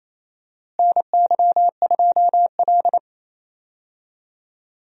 Morse code, heard as a keyed tone, about 28 wpm, 700 Hz. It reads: NY2L
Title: USA Call Signs - 4 Characters - Repeated 3x - 28wpm